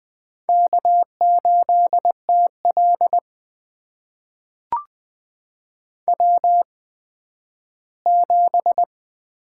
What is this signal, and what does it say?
Morse code, heard as a keyed tone, about 20 wpm, spaced as for 12 wpm, 700 Hz. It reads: K8TL W7
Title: USA Call Signs - 4 Characters - 8x Character Spacing - 20wpm